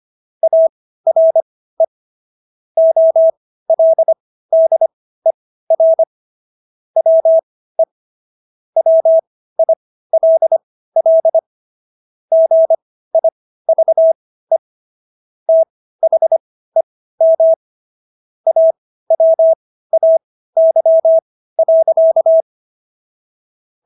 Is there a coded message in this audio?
Yes: Morse code from 0.4 to 22.4 s, heard as a keyed tone, about 25 wpm, spaced as for 15 wpm, 650 Hz. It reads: ARE OLDER WE WILL GIVE THEM AWAY.